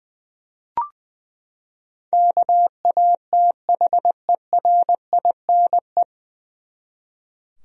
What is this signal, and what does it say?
Morse code, heard as a keyed tone, about 21 wpm, 700 Hz. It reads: KATHERINE
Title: USA Common Names - 20wpm